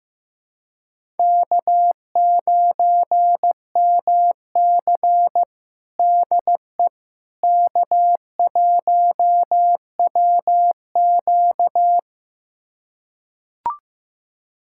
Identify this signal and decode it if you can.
Morse code, heard as a keyed tone, about 15 wpm, 700 Hz. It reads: K9MC DE K1WQ
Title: USA Callsign DE Callsign - 4 Characters - 15wpm